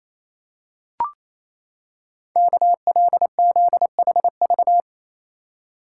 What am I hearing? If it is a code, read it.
Morse code, heard as a keyed tone, about 27 wpm, 700 Hz. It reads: KLZHV
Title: Code Groups - 5 Letters - 28wpm